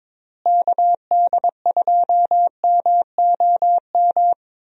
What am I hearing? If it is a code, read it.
Morse code, heard as a keyed tone, about 22 wpm, 700 Hz. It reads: KD2MOM
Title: USA Call Signs - 6 Characters - Repeated 3x - 22wpm